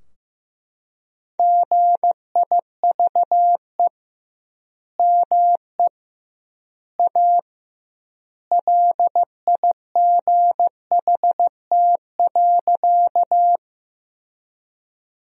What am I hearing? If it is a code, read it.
Morse code, heard as a keyed tone, about 16 wpm, 700 Hz. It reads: GIVE ME A LIGHT.
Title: 2x Word-Spacing - Sentences from Top 200 Words - 15wpm